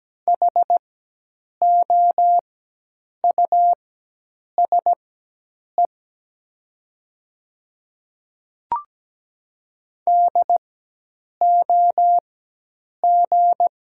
Morse code, heard as a keyed tone, about 17 wpm, spaced as for 8 wpm, 700 Hz. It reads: HOUSE DOG
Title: 4x Character Spacing - Top 500 Words - 17wpm